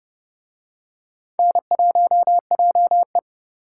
Morse code, heard as a keyed tone, about 30 wpm, 700 Hz. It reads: N1JE